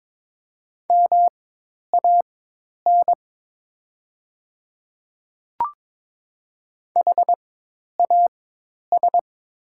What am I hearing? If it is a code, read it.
Morse code, heard as a keyed tone, about 22 wpm, spaced as for 10 wpm, 700 Hz. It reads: MAN HAS
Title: 4x Character Spacing - Top 500 Words - 22wpm